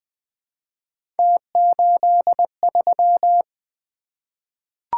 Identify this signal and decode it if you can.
Morse code, heard as a keyed tone, about 20 wpm, 700 Hz. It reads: T83
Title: Three Letter-Number Combinations - 20wpm